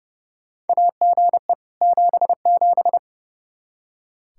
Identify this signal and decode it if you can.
Morse code, heard as a keyed tone, about 30 wpm, 700 Hz. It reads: AGE 77
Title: Practice QSO - Exchange 3 - More Info - 30wpm